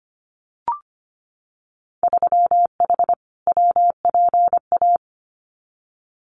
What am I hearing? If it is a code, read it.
Morse code, heard as a keyed tone, about 25 wpm, 700 Hz. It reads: 3H WPA
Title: Winter Field Day Exchanges 25wpm